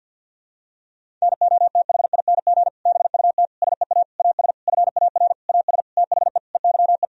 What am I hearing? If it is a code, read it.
Morse code, heard as a keyed tone, about 51 wpm, 700 Hz. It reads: NOTHING BUT SEA AS FAR AS THE EYE
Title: Sentences from Top 1000 Words - Review Part 1 - 50wpm